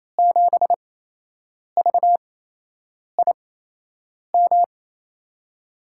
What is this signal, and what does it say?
Morse code, heard as a keyed tone, about 28 wpm, spaced as for 8 wpm, 700 Hz. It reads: 7VIM
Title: USA Call Signs - 6 Characters - 8x Character Spacing - 28wpm